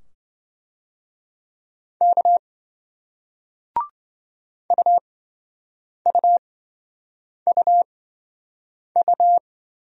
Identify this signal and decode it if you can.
Morse code, heard as a keyed tone, about 27 wpm, spaced as for 7 wpm, 700 Hz. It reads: K UUUU